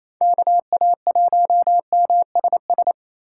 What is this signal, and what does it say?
Morse code, heard as a keyed tone, about 28 wpm, 700 Hz. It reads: KA1MSS
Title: USA Call Signs - 8x Character Spacing - 28wpm